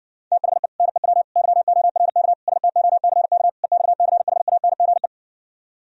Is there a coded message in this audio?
Yes: Morse code from 0.3 to 5.1 s, heard as a keyed tone, about 60 wpm, 700 Hz. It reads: THE NEW YORK STOCK EXCHANGE